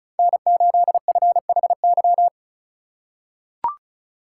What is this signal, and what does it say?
Morse code, heard as a keyed tone, about 36 wpm, 690 Hz. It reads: N8FHY
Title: USA Call Signs - 5 Characters - 3x Character Spacing - 35wpm